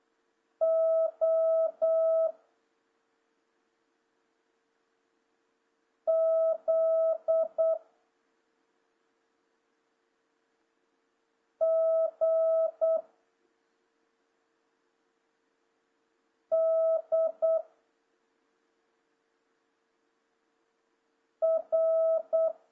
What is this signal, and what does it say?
Morse code, heard as a keyed tone, about 8 wpm, 650 Hz. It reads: O Z G D R
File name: jh8cbh / 2026cw